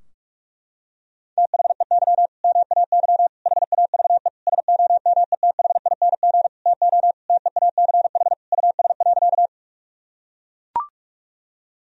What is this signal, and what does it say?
Morse code, heard as a keyed tone, about 42 wpm, 700 Hz. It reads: THEY MAY HAVE SOMETHING TO TEACH US.
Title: Sentences from Top 1000 Words - Review Part 2 - 45wpm